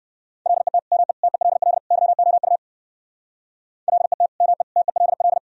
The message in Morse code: LET ME TELL YOU LET ME TELL

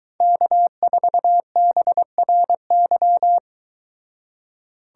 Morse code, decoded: K4BRY

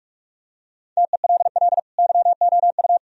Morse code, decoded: TELL YOU

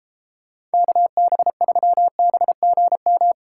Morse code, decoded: KB3BGM

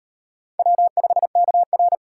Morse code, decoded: W5KR

